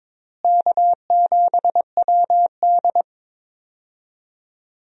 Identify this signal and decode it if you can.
Morse code, heard as a keyed tone, about 22 wpm, 700 Hz. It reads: K7WD
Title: USA Call Signs - 4 Characters - Repeated 3x - 22wpm